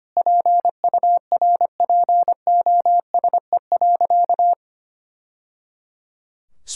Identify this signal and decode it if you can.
Morse code, heard as a keyed tone, about 25 wpm, 700 Hz. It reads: PURPOSE.
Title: Sentences from Top 1000 Words - Review Part 6 - 25wpm